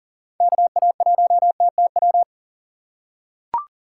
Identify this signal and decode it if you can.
Morse code, heard as a keyed tone, about 40 wpm, 700 Hz. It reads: KA1TTW
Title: USA Call Signs - 6 Characters - 3x Character Spacing - 40wpm